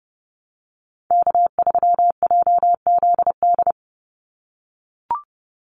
Morse code, heard as a keyed tone, about 30 wpm, 700 Hz. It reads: K3JZD
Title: NA SOTA Chaser Callsigns 30wpm